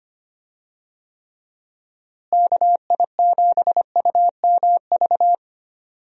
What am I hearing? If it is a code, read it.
Morse code, heard as a keyed tone, about 25 wpm, 700 Hz. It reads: KI7UMV
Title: USA Call Signs - 6 Characters - Repeated 3x - 25wpm